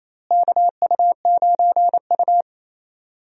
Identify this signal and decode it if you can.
Morse code, heard as a keyed tone, about 28 wpm, 700 Hz. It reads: KU9U
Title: USA Call Signs - 4 Characters - Repeated 3x - 28wpm